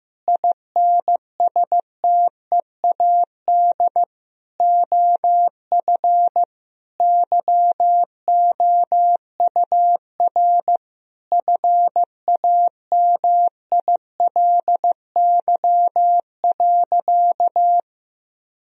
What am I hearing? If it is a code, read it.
Morse code, heard as a keyed tone, about 15 wpm, 700 Hz. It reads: INSTEAD OF YOUR FAMILY.
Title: Sentences from Top 1000 Words - Review Part 4 - 15wpm